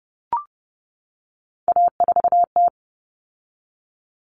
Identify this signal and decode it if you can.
Morse code, heard as a keyed tone, about 30 wpm, 700 Hz. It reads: A4T
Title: All Cut Numbers - 3-Digit 30wpm